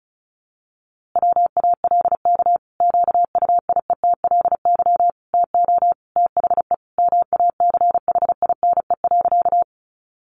Morse code, heard as a keyed tone, about 34 wpm, 700 Hz. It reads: WALK QUIETLY TO THE MACHINE.
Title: A Sound Of Thunder 35wpm